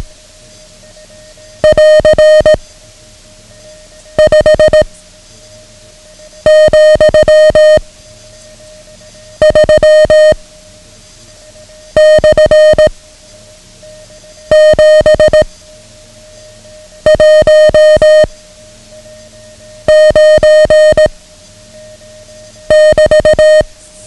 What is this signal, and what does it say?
Morse code, heard as a keyed tone, about 18 wpm, 620 Hz. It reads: <AR> 5 , 3 / 7 1 9 <BT>
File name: ham radio / CodeQuick / Code Quick 09